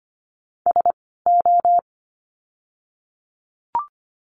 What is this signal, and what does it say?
Morse code, heard as a keyed tone, about 25 wpm, spaced as for 16 wpm, 700 Hz. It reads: SO
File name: course / 25:10wpm / 094 Lesson 16f - Letter c - Words 25wpm